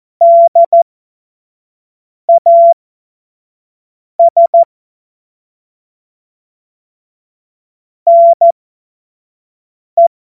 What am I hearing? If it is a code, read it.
Morse code, heard as a keyed tone, about 14 wpm, spaced as for 5 wpm, 680 Hz. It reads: DAS NE
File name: Morsekurs / Witze / Witz 12. Oktober 2021, 25 F70